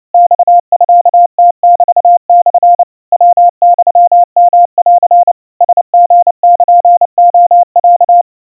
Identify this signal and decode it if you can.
Morse code (Japanese Wabun code), heard as a keyed tone, about 29 wpm, 700 Hz. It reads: マミムメモ ヤユヨン ラリルレロ
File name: aiueo 100 cpm 700 Hz Crispy